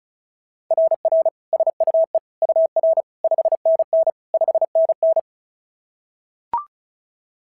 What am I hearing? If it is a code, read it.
Morse code, heard as a keyed tone, about 36 wpm, 690 Hz. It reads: RR SUE UR 5NN 5NN
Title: SOTA QSO Practice 35wpm